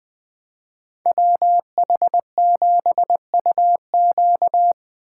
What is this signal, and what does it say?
Morse code, heard as a keyed tone, about 20 wpm, 700 Hz. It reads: WH7UQ
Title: USA Call Signs - 6x Character Spacing - 20wpm